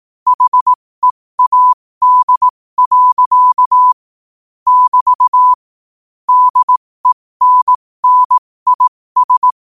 Morse code, heard as a keyed tone, about 18 wpm, spaced as for 15 wpm, 1000 Hz. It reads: HEAD. <BT> DENNIS